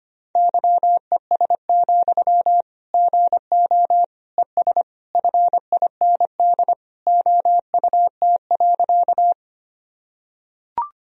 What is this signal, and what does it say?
Morse code, heard as a keyed tone, about 25 wpm, 700 Hz. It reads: YES, GO ES FIND OUT.
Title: Sentences from Top 500 Words and Abbreviations 25wpm